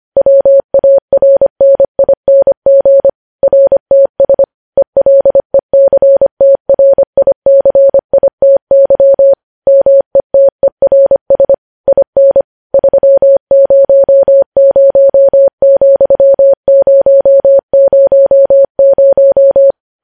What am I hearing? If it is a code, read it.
Morse code, heard as a keyed tone, about 25 wpm, 550 Hz. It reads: WARNING RTS ELECTRICITY METERS IN 300,000
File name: podcast / News headlines-1x-25x25